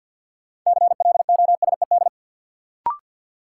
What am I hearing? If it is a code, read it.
Morse code, heard as a keyed tone, about 48 wpm, 700 Hz. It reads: CLOSED
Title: Sentences from Top 1400 Words - New Word Drill - 50wpm